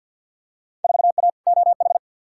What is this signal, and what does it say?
Morse code, heard as a keyed tone, about 53 wpm, 700 Hz. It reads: 4A OH
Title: Field Day Exchanges - 50wpm